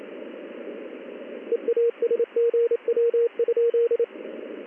Morse code, heard as a keyed tone, about 28 wpm, 470 Hz, with radio noise. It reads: EASGW?
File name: contests / ss-ssb-k5tr- -1700